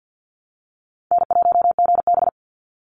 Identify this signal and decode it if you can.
Morse code, heard as a keyed tone, about 48 wpm, 700 Hz. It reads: N1GB
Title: NA SOTA Chaser Callsigns 50wpm